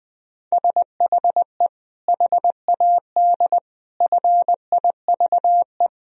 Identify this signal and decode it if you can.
Morse code, heard as a keyed tone, about 21 wpm, 700 Hz. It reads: SHE HAD FIVE
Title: Sentences from Top 1000 Words - Review Part 2 - 20wpm